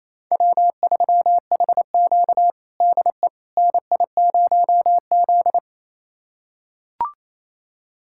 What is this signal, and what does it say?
Morse code, heard as a keyed tone, about 28 wpm, 700 Hz. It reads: W3HQ DE NI0Z
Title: USA Callsign DE Callsign - 4 Characters - 28wpm